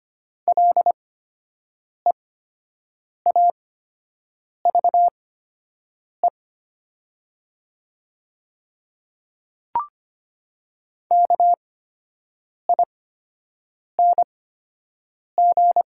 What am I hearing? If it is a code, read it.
Morse code, heard as a keyed tone, about 25 wpm, spaced as for 7 wpm, 700 Hz. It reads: LEAVE KING
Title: 8x Character Spacing - Top 500 Words - 25wpm